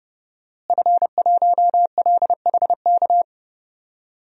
Morse code, heard as a keyed tone, about 30 wpm, 700 Hz. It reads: F1LHK
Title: DXCC-100 Call Signs - 5 Characters - Repeated 2x - 30wpm